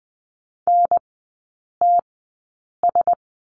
Morse code, heard as a keyed tone, about 20 wpm, spaced as for 8 wpm, 700 Hz. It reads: NTS
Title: 041 Lesson 8e - Number 1 - Three Characters 20wpm